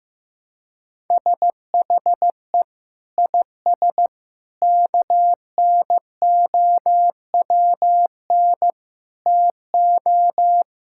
Morse code, heard as a keyed tone, about 15 wpm, 700 Hz. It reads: SHE IS KNOWN TO
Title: Sentences from Top 1000 Words - Review Part 5 - 15wpm